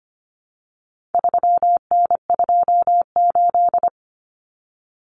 Morse code, heard as a keyed tone, about 25 wpm, 700 Hz. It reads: 3N28
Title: Common Cut Numbers - 4-Digit 25wpm